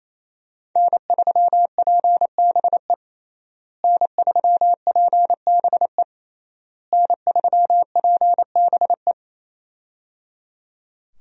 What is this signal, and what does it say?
Morse code, heard as a keyed tone, about 28 wpm, 700 Hz. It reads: N3PBE N3PBE N3PBE
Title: USA Call Signs - 5 Characters - Repeated 3x - 28wpm